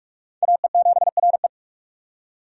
Morse code, heard as a keyed tone, about 55 wpm, 700 Hz. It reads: AE7RE
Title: Speed Racing - USA 2x2 Call Signs 45wpm